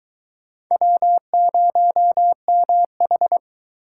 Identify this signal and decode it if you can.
Morse code, heard as a keyed tone, about 20 wpm, 700 Hz. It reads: W0MH